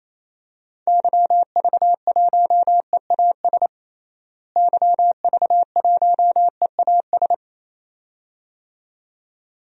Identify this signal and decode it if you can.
Morse code, heard as a keyed tone, about 28 wpm, 700 Hz. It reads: YV1EAS YV1EAS
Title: DXCC-100 Call Signs - 6 Characters - Repeated 2x - 28wpm